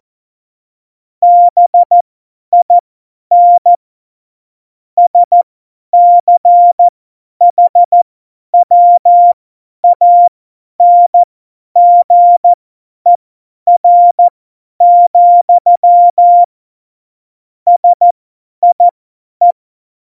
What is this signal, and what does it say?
Morse code, heard as a keyed tone, about 14 wpm, spaced as for 10 wpm, 700 Hz. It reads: BIN SCHWANGER, SIE
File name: Morsekurs / Witze / Witz 06. Februar 2018, 50 F70